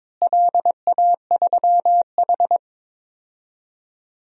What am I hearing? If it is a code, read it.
Morse code, heard as a keyed tone, about 22 wpm, 700 Hz. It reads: LA3H
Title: DXCC-100 Call Signs - 4 Characters - Repeated 3x - 22wpm